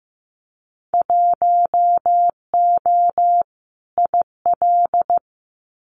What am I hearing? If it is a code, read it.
Morse code, heard as a keyed tone, about 15 wpm, 700 Hz. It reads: 1O IL
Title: Winter Field Day Exchanges 15wpm